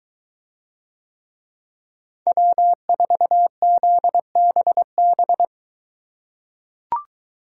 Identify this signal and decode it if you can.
Morse code, heard as a keyed tone, about 23 wpm, 700 Hz. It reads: W4ZBB